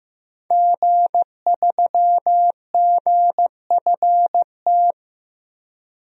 Morse code, heard as a keyed tone, about 16 wpm, 700 Hz. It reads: G3GFT
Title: DXCC-100 Call Signs - 5 Characters - Repeated 2x - 15wpm